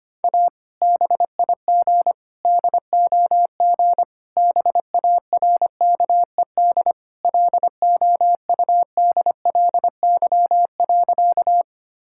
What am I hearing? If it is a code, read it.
Morse code, heard as a keyed tone, about 25 wpm, 700 Hz. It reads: A BIG DOG BARKED LOUDLY.
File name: practice / 25wpm / Sentences with 2-Word Phrases - 25wpm